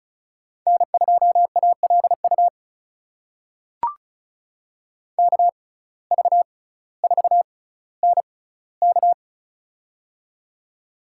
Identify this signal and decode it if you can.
Morse code, heard as a keyed tone, about 36 wpm, spaced as for 18 wpm, 700 Hz. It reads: N2ALU KV4NK
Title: USA Call Signs - 5 Characters - 6x Character Spacing - 35wpm